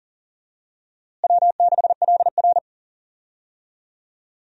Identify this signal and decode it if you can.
Morse code, heard as a keyed tone, about 37 wpm, 700 Hz. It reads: W6LR
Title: USA Call Signs - 4 Characters - Repeated 3x - 40wpm